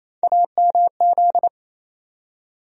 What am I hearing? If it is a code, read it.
Morse code, heard as a keyed tone, about 28 wpm, 700 Hz. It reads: AMZ
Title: USA Call Sign Postfixes - 3 Letters 28wpm